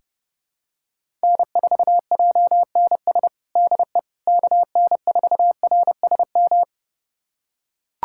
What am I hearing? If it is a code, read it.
Morse code, heard as a keyed tone, about 30 wpm, 700 Hz. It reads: N4JNS DE KN4RSM
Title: USA Callsign DE Callsign 30wpm